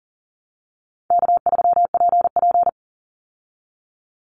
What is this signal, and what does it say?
Morse code, heard as a keyed tone, about 40 wpm, 700 Hz. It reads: K3PP
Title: Easy Call Signs Repeated 3x - 40wpm